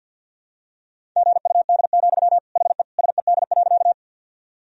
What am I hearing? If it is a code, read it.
Morse code, heard as a keyed tone, about 50 wpm, 700 Hz. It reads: GUD, HE SED.
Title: Sentences from Top 200 Words and Abbreviations 50wpm